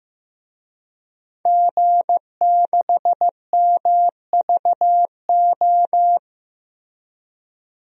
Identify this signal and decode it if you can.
Morse code, heard as a keyed tone, about 15 wpm, 700 Hz. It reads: G6MVO
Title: DXCC-100 Call Signs - 5 Characters - Repeated 2x - 15wpm